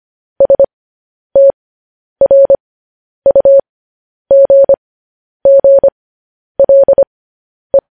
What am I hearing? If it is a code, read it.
Morse code, heard as a keyed tone, about 25 wpm, spaced as for 10 wpm, 550 Hz. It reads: STRUGGLE
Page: news headlines in morse code at 10 wpm